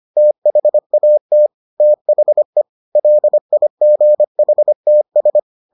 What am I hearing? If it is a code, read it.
Morse code, heard as a keyed tone, about 25 wpm, 600 Hz. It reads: THAT THE LIGHTS